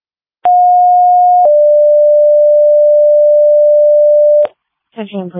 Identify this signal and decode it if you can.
Two-tone page, heard as a one-second tone, then a three-second tone, about 720 then 590 Hz. It is heard from 0.4 to 4.5 s.